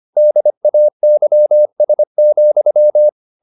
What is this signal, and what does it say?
Morse code, heard as a keyed tone, about 25 wpm, 600 Hz. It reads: DAYS,